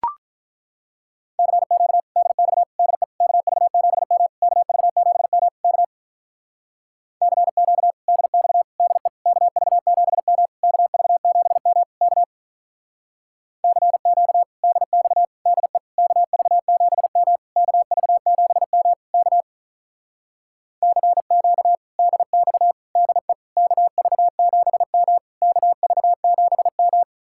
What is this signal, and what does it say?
Morse code, heard as a keyed tone, about 43 wpm, 700 Hz. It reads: CQ DX DE KV7M KV7M K CQ DX DE KV7M KV7M K CQ DX DE KV7M KV7M K CQ DX DE KV7M KV7M
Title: Speed-Racing Common QSO Phrases - 35wpm